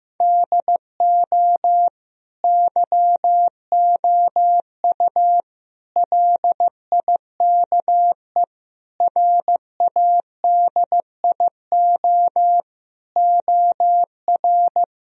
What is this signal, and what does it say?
Morse code, heard as a keyed tone, about 15 wpm, 700 Hz. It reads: DO YOU LIKE RADIO OR
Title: Sentences from Top 1000 Words - Review Part 4 - 15wpm